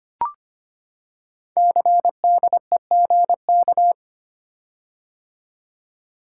Morse code, heard as a keyed tone, about 25 wpm, 700 Hz. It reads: CDEGK
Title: Code Groups - 5 Letters - 25wpm